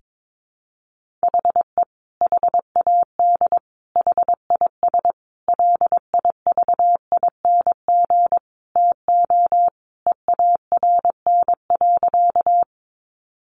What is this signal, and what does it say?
Morse code, heard as a keyed tone, about 22 wpm, 700 Hz. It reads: HE HAD HIS LIVING TO EARN.
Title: Sentences from Top 1500 Words 22wpm